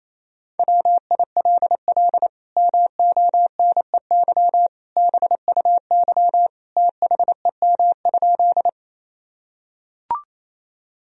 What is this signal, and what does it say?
Morse code, heard as a keyed tone, about 28 wpm, 700 Hz. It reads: WILL MONEY BUY THEM?
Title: Sentences from Top 1000 Words - Review Part 4 - 28wpm